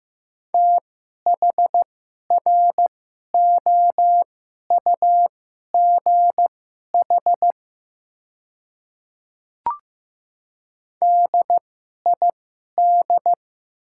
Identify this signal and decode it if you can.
Morse code, heard as a keyed tone, about 15 wpm, spaced as for 11 wpm, 700 Hz. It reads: THROUGH DID
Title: 2x Character Spacing - Top 200 Words - 15wpm